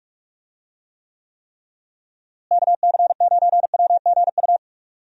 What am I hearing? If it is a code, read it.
Morse code, heard as a keyed tone, about 46 wpm, 700 Hz. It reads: KC9WGU
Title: USA Call Signs - 6 Characters - Repeated 3x - 45wpm